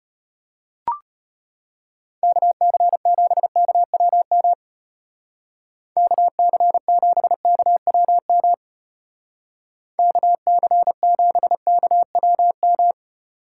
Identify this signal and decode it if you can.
Morse code, heard as a keyed tone, about 30 wpm, 700 Hz. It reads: KC7KWM KC7KWM KC7KWM
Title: Speed Racing - USA 2x3 Call Signs 25wpm